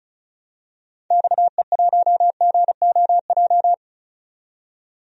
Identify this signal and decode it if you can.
Morse code, heard as a keyed tone, about 36 wpm, 700 Hz. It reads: XE1GOJ
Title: DXCC-100 Call Signs - 6 Characters - 35wpm